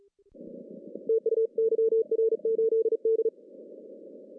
Morse code, heard as a keyed tone, about 37 wpm, 440 Hz. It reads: TU YR8D
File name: CW / cqww-cw-k5tr